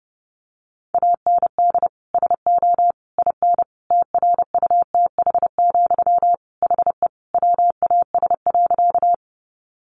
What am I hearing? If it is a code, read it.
Morse code, heard as a keyed tone, about 30 wpm, 700 Hz. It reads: AND SO IN TRUTH, HE WAS.
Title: Sentences from Top 800 Words 30wpm